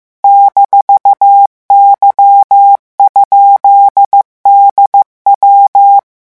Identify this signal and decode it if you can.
Morse code, heard as a keyed tone, about 15 wpm, 800 Hz. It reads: -Y?DW
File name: morse mp3 / revision17-vitesse 14 mots